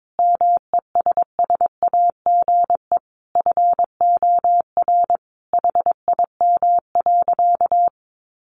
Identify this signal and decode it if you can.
Morse code, heard as a keyed tone, about 22 wpm, 700 Hz. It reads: MESSAGE FOR HIM.
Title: Sentences from Top 1100 Words 22wpm